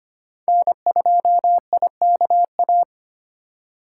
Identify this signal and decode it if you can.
Morse code, heard as a keyed tone, about 25 wpm, 700 Hz. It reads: N2IKA
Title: USA Call Signs - 5 Characters - 4x Character Spacing - 25wpm